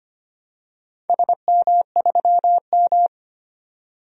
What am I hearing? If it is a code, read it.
Morse code, heard as a keyed tone, about 25 wpm, 700 Hz. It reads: SM3M
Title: DXCC-100 Call Signs - 4 Characters - 25wpm